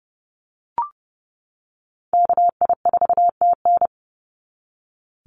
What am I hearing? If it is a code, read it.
Morse code, heard as a keyed tone, about 30 wpm, 700 Hz. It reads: KI4TN
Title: NA SOTA Activator Callsigns 30wpm